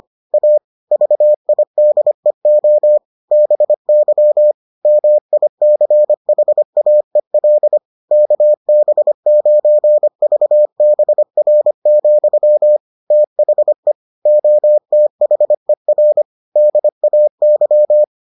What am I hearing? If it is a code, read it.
Morse code, heard as a keyed tone, about 25 wpm, 600 Hz. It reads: A VIDEO BY MICHAEL KB9VBR, THE OTHER DAY